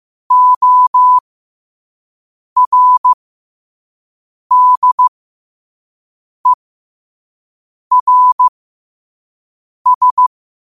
Morse code, heard as a keyed tone, about 15 wpm, spaced as for 5 wpm, 1000 Hz. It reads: ORDERS